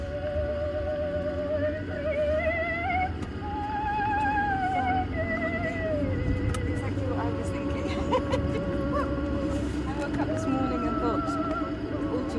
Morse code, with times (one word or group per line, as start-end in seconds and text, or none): none